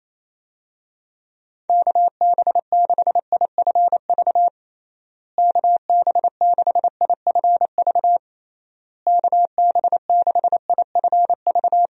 Morse code, heard as a keyed tone, about 28 wpm, 700 Hz. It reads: KB6IFV KB6IFV KB6IFV
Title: USA Call Signs - 6 Characters - Repeated 3x - 28wpm